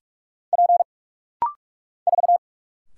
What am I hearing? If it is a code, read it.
Morse code, heard as a keyed tone, about 46 wpm, 700 Hz. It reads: P 4
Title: Single Letter-Number Alt - Rapid-Fire 45wpm